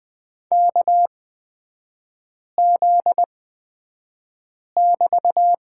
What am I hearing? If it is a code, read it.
Morse code, heard as a keyed tone, about 20 wpm, spaced as for 5 wpm, 700 Hz. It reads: KZ<BT>
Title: 243 Lesson 39e - Letter z - Three Characters 20wpm